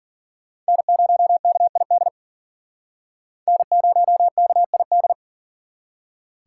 Morse code, heard as a keyed tone, about 52 wpm, 700 Hz. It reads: N0KID N0KID